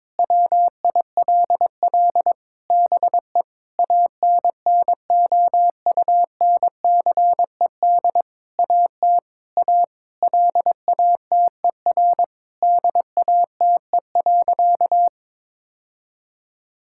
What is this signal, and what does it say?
Morse code, heard as a keyed tone, about 22 wpm, 700 Hz. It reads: WILL BE ANNOUNCED AT A LATER DATE.
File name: Sentences with 7-Word Phrases - 22wpm